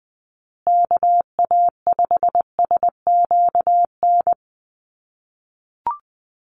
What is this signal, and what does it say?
Morse code, heard as a keyed tone, about 20 wpm, 700 Hz. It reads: KA5SQN
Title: USA Call Signs Repeated 3x 20wpm